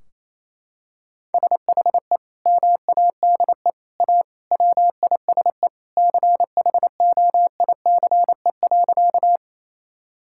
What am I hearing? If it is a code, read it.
Morse code, heard as a keyed tone, about 28 wpm, 700 Hz. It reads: SHE MADE A WISE CHOICE.
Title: Sentences from Top 1600 Words 28wpm